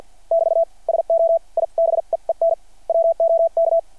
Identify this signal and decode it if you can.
Morse code, heard as a keyed tone, about 52 wpm, spaced as for 44 wpm, 660 Hz. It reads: <BT> SO I BEEN WOK
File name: archived / CRSnet